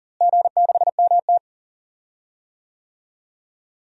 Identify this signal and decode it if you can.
Morse code, heard as a keyed tone, about 38 wpm, 700 Hz. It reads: G6MT
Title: DXCC-100 Call Signs - 4 Characters - Repeated 3x - 40wpm